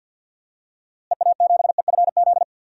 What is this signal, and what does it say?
Morse code, heard as a keyed tone, about 53 wpm, 700 Hz. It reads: EA7EFZ
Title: DXCC-100 Call Signs - 6 Characters - Repeated 2x - 50wpm